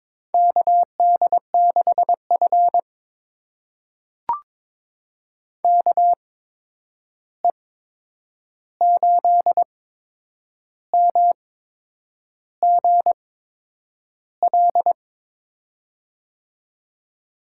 Morse code, heard as a keyed tone, about 23 wpm, spaced as for 8 wpm, 700 Hz. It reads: KD6F KE8MGL
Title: USA Call Signs - 8x Character Spacing - 22wpm